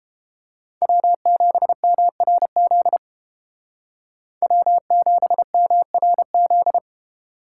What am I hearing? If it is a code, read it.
Morse code, heard as a keyed tone, about 30 wpm, 700 Hz. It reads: W7MRZ W7MRZ